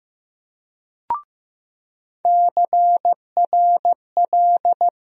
Morse code, heard as a keyed tone, about 15 wpm, 700 Hz. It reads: CRL
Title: USA Call Sign Postfixes - 3 Letters 15wpm